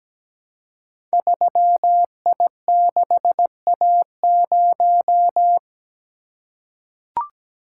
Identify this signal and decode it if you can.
Morse code, heard as a keyed tone, about 17 wpm, 700 Hz. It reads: 3I6A0
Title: Code Groups - 5 Letter:Numbers - 17wpm